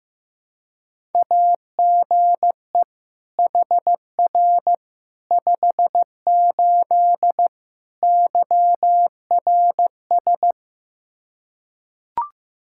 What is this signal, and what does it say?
Morse code, heard as a keyed tone, about 16 wpm, 700 Hz. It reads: AGE HR 58 YRS
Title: Practice QSO - Exchange 3 - More Info - 15wpm